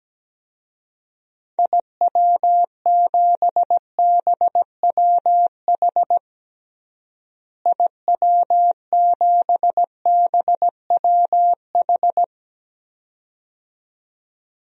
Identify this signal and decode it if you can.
Morse code, heard as a keyed tone, about 17 wpm, 700 Hz. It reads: IW7BWH IW7BWH
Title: DXCC-100 Call Signs - 6 Characters - Repeated 2x - 17wpm